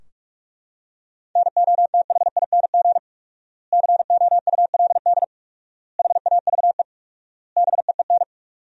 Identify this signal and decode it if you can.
Morse code, heard as a keyed tone, about 41 wpm, 700 Hz. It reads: NOTHING COULD HAVE BEEN
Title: 4x Word-Spacing - Sentences from Top 500 Words - 45wpm